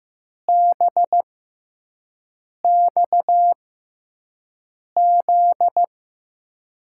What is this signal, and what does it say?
Morse code, heard as a keyed tone, about 15 wpm, spaced as for 5 wpm, 700 Hz. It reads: BXZ
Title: USA Call Signs - 6 Characters - 6x Character Spacing - 15wpm